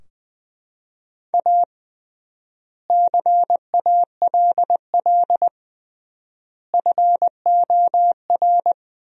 Morse code, heard as a keyed tone, about 20 wpm, 700 Hz. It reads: A CALL FOR
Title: 3x Word-Spacing - Sentences from Top 300 Words - 20wpm